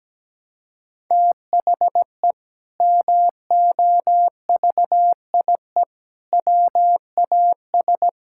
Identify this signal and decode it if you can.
Morse code, heard as a keyed tone, about 17 wpm, 700 Hz. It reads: THE MOVIE WAS
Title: Sentences from Top 1000 Words - Review Part 5 - 17wpm